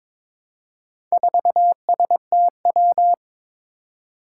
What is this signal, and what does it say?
Morse code, heard as a keyed tone, about 22 wpm, 700 Hz. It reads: 4STW
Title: Four Challenging Letter-Number Combinations 22wpm